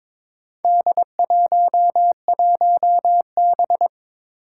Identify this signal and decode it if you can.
Morse code, heard as a keyed tone, about 23 wpm, 700 Hz. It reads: D11B
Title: Four Challenging Letter-Number Combinations 22wpm